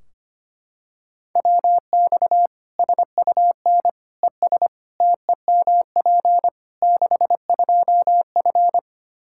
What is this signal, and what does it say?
Morse code, heard as a keyed tone, about 25 wpm, 700 Hz. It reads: WX SUN ES TEMP 62F
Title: Practice QSO - Exchange 2 - Info - 25wpm